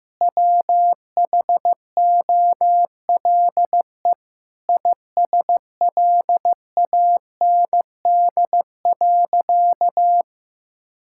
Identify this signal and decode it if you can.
Morse code, heard as a keyed tone, about 15 wpm, 700 Hz. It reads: WHOLE ISLAND.
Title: Sentences from Top 1000 Words - Review Part 5 - 15wpm